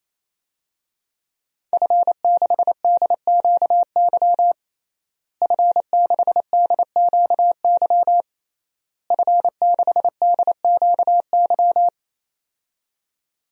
Morse code, heard as a keyed tone, about 27 wpm, 700 Hz. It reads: F6DQY F6DQY F6DQY
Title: DXCC-100 Call Signs - 5 Characters - Repeated 3x - 28wpm